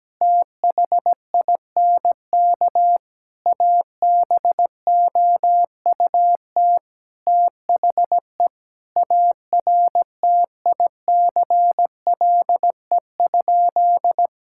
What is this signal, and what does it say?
Morse code, heard as a keyed tone, about 17 wpm, 700 Hz. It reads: THINK ABOUT THE ARTICLE?
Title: Sentences from Top 1000 Words - Review Part 5 - 17wpm